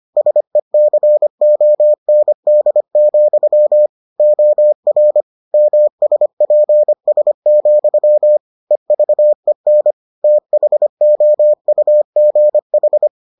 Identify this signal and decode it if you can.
Morse code, heard as a keyed tone, about 25 wpm, 600 Hz. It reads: SECOND, OR MSPS, EVEN THOUGH